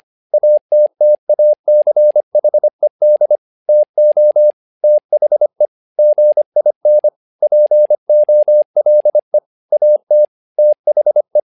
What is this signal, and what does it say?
Morse code, heard as a keyed tone, about 25 wpm, 600 Hz. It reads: ATTACHED TO THE GIN POLE AT THE
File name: learning center / tower trouble / tower trouble 25wpm